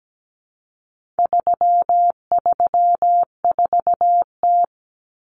Morse code, heard as a keyed tone, about 17 wpm, 700 Hz. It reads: 334T
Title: Common Cut Numbers - 4-Digit 17wpm